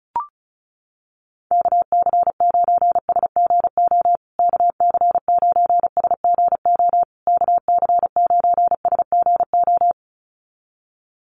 Morse code, heard as a keyed tone, about 34 wpm, 700 Hz. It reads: KC9SGO KC9SGO KC9SGO
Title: USA Call Signs Repeated 3x 35wpm